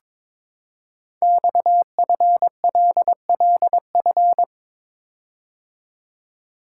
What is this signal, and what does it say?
Morse code, heard as a keyed tone, about 23 wpm, 700 Hz. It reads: XFLLF
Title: Code Groups - 5 Letters - 22wpm